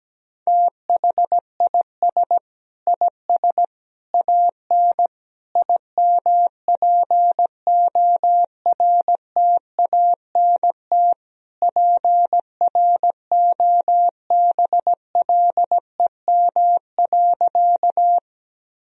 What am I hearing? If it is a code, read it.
Morse code, heard as a keyed tone, about 17 wpm, 700 Hz. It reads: THIS IS AN IMPORTANT PROBLEM.
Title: Sentences from Top 1000 Words - Review Part 2 - 17wpm